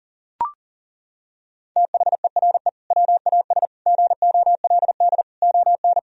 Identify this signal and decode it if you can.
Morse code, heard as a keyed tone, about 40 wpm, 700 Hz. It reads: THERE WAS GOLD ON
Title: Sentences from Top 1000 Words - Review Part 4 - 40wpm